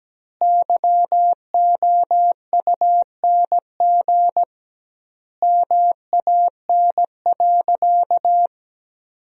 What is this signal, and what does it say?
Morse code, heard as a keyed tone, about 17 wpm, 700 Hz. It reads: YOUNG MAN.
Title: 2x Word-Spacing - Sentences from Top 400 Words - 17wpm